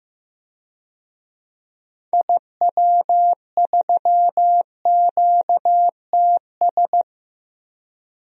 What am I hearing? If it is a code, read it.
Morse code, heard as a keyed tone, about 15 wpm, 700 Hz. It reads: IW3QTS
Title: DXCC-100 Call Signs - 6 Characters - 15wpm